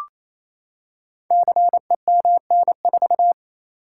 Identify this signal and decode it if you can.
Morse code, heard as a keyed tone, about 28 wpm, 700 Hz. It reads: CEMN4
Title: Code Groups - 5 Letter:Numbers - 28wpm